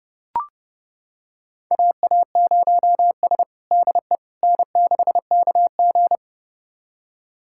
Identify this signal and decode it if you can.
Morse code, heard as a keyed tone, about 30 wpm, 700 Hz. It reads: AA0S DE N6KG
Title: USA Callsign DE Callsign - 4 Characters - 30wpm